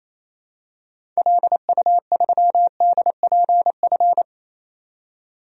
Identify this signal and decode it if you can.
Morse code, heard as a keyed tone, about 28 wpm, 700 Hz. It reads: LU3DPF
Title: DXCC-100 Call Signs - 6 Characters - 28wpm